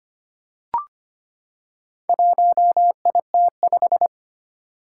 Morse code, heard as a keyed tone, about 25 wpm, 700 Hz. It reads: 1IT5
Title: Four Challenging Letter-Number Combinations 25wpm